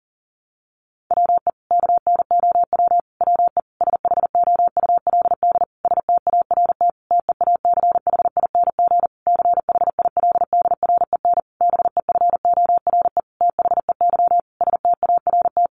Morse code, heard as a keyed tone, about 40 wpm, 700 Hz. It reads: WE KNOW WE SHOULD START TEACHING CHILDREN BEFORE THEY START
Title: Sentences from Top 1100 Words 40wpm